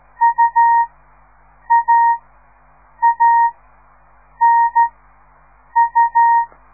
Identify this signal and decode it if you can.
Morse code, heard as a keyed tone, about 13 wpm, spaced as for 7 wpm, 930 Hz. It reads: UAANU
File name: radiomorningstar / morse / les 4